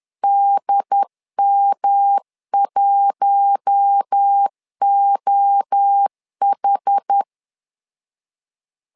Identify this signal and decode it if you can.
Morse code, heard as a keyed tone, about 10 wpm, 800 Hz. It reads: DM1OH
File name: LEZIONI / 4 Q1 10wpm 2448